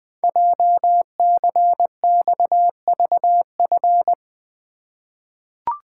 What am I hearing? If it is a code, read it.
Morse code, heard as a keyed tone, about 20 wpm, 700 Hz. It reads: JCXVF
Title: Code Groups - 5 Letters - 20wpm